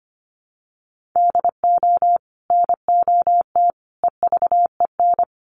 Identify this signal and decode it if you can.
Morse code, heard as a keyed tone, about 25 wpm, 700 Hz. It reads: DO NOT EVEN
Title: Sets of 5 Words Encore 25wpm